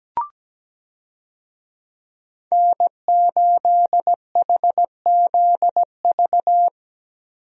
Morse code, heard as a keyed tone, about 17 wpm, 700 Hz. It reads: N8HZV
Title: USA Call Signs - 5 Characters - Repeated 3x - 17wpm